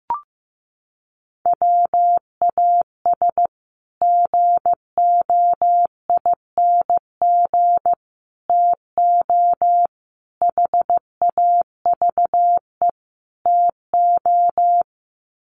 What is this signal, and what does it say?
Morse code, heard as a keyed tone, about 15 wpm, 700 Hz. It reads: WAS GOING TO HAVE TO
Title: Sets of 5 Words Encore 15wpm